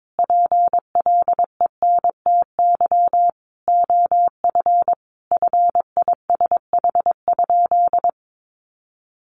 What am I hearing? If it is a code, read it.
Morse code, heard as a keyed tone, about 22 wpm, 700 Hz. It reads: PLENTY OF FISH?
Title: Sentences from Top 1500 Words 22wpm